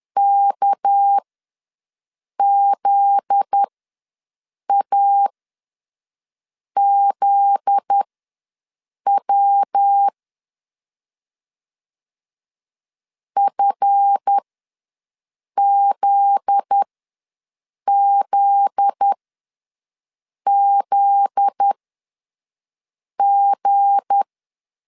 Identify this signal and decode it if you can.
Morse code, heard as a keyed tone, about 10 wpm, spaced as for 5 wpm, 800 Hz. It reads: KZAZW FZZZG
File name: LEZIONI / 7 GF9Z / 7 GF9Z ws 10wpm 3919